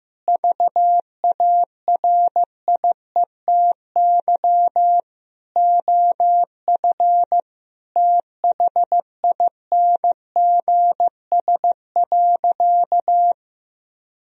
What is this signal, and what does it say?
Morse code, heard as a keyed tone, about 15 wpm, 700 Hz. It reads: VARIETY OF THINGS.
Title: Sentences from Top 1000 Words - Review Part 4 - 15wpm